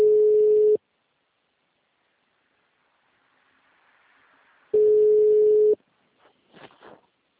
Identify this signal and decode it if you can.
Ringback tone (Europe, 425 Hz).